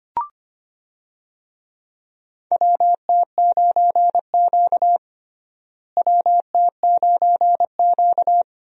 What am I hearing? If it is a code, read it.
Morse code, heard as a keyed tone, about 25 wpm, 700 Hz. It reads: WT9Q WT9Q
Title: USA Call Signs - 4 Characters - Repeated 3x - 25wpm